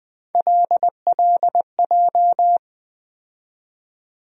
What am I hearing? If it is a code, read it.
Morse code, heard as a keyed tone, about 20 wpm, 700 Hz. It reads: LLJ
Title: USA Call Sign Postfixes - 3 Letters 20wpm